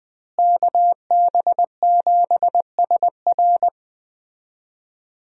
Morse code, heard as a keyed tone, about 20 wpm, 700 Hz. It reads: KB7SR
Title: USA Call Signs - 5 Characters - Repeated 3x - 20wpm